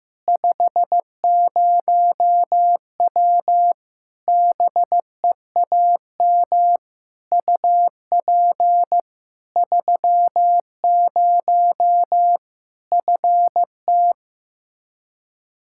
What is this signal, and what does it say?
Morse code, heard as a keyed tone, about 15 wpm, 700 Hz. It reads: 50W BEAM UP 30 FT
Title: Practice QSO - Exchange 2 - Info - 15wpm